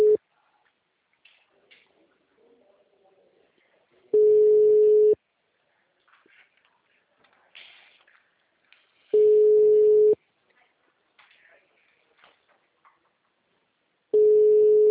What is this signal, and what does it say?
Ringback tone (Europe, 425 Hz).
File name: out- -2011- - - .0